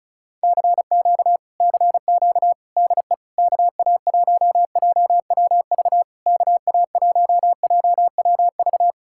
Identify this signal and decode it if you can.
Morse code, heard as a keyed tone, about 40 wpm, 700 Hz. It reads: CQ CQ DE KA1JWV KA1JWV